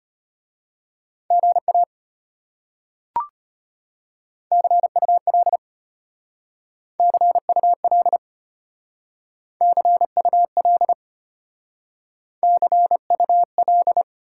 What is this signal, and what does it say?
Morse code, heard as a keyed tone, about 34 wpm, 700 Hz. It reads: GA CUL CUL CUL CUL